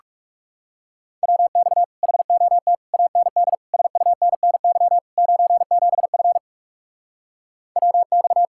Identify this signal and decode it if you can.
Morse code, heard as a keyed tone, about 40 wpm, 700 Hz. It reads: WX HOT AND SUNNY 97F WX